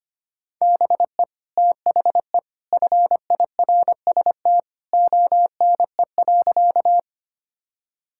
Morse code, heard as a keyed tone, about 25 wpm, 700 Hz. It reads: BE THE FIRST ONE.
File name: practice / 25wpm / Sentences with 2-Word Phrases - 25wpm